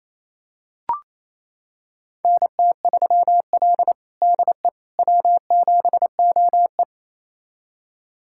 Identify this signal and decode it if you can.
Morse code, heard as a keyed tone, about 27 wpm, 690 Hz. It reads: NT3L DE W7OE
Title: USA Callsign DE Callsign - 4 Characters - 28wpm